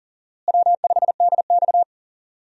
Morse code, heard as a keyed tone, about 37 wpm, 700 Hz. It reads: W5DX
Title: USA Call Signs - 4 Characters - Repeated 2x - 40wpm